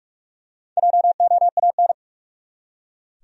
Morse code, heard as a keyed tone, about 46 wpm, 700 Hz. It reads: JOAN
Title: USA Common Names - Max 4 Letters - 45wpm